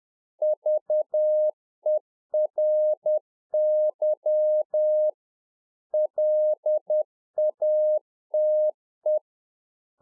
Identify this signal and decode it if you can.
Morse code, heard as a keyed tone, about 10 wpm, 610 Hz. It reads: VERY LATE
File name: 2011-04-mp3 / A SMALL TRANCEIVER 1 10WPM